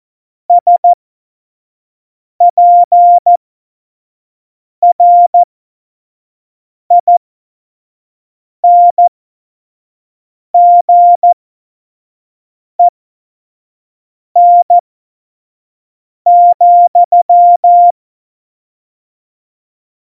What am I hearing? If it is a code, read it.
Morse code, heard as a keyed tone, about 14 wpm, spaced as for 5 wpm, 700 Hz. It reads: SPRINGEN,
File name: Morsekurs / Witze / Witz 14. August 2018, 25 F70